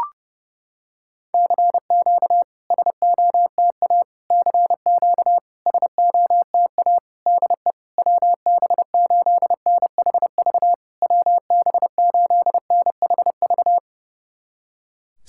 Morse code, heard as a keyed tone, about 30 wpm, 700 Hz. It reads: CQ SOTA CQ SOTA DE WB8NHV WB8NHV
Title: SOTA QSO Practice 30wpm